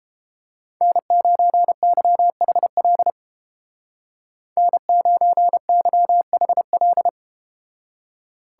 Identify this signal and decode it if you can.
Morse code, heard as a keyed tone, about 31 wpm, 700 Hz. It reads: N9YHL N9YHL